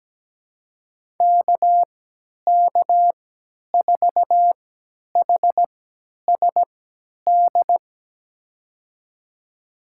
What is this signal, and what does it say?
Morse code, heard as a keyed tone, about 17 wpm, spaced as for 10 wpm, 700 Hz. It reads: KK4HSD
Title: USA Call Signs - 6 Characters - 3x Character Spacing - 17wpm